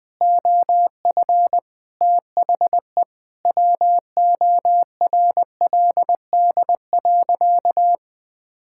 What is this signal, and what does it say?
Morse code, heard as a keyed tone, about 20 wpm, 700 Hz. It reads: OF THE WORLD.
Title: Sentences with 5-Word Phrases - Spoken Beginning - 20wpm